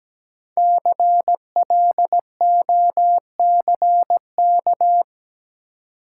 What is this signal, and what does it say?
Morse code, heard as a keyed tone, about 17 wpm, 700 Hz. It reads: CLOCK
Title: Five Letter Words - 17wpm